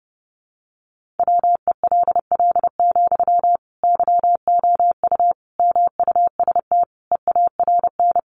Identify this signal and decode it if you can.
Morse code, heard as a keyed tone, about 30 wpm, 700 Hz. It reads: WELL, YOU MUST EARN
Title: Sentences from Top 1500 Words 30wpm